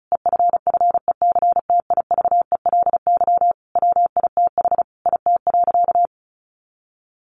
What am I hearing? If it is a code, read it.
Morse code, heard as a keyed tone, about 34 wpm, 700 Hz. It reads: EFFECTIVELY WITH IT.
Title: Sentences from Top 1400 Words 35wpm